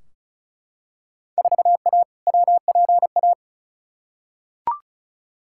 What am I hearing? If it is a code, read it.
Morse code, heard as a keyed tone, about 36 wpm, 700 Hz. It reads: 4A WPA
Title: Field Day Exchanges Encore - 35wpm